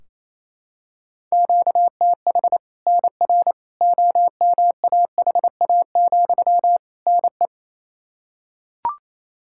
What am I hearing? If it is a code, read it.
Morse code, heard as a keyed tone, about 28 wpm, 700 Hz. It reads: QTH NR OMAHA, NE
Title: Practice QSO - Exchange 1 - Beginning - 28wpm